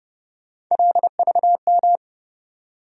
Morse code, heard as a keyed tone, about 31 wpm, 700 Hz. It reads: LVM
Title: USA Call Sign Postfixes - 3 Letters 30wpm